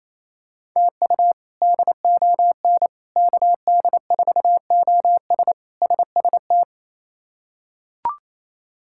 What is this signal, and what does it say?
Morse code, heard as a keyed tone, about 28 wpm, 700 Hz. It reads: TU DON KD4OS SST
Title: K1USN SST QSO Practice - 28wpm